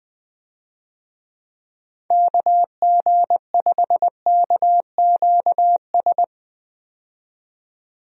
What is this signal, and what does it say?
Morse code, heard as a keyed tone, about 21 wpm, 700 Hz. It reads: KG5KQS
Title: USA Call Signs - 6 Characters - 20wpm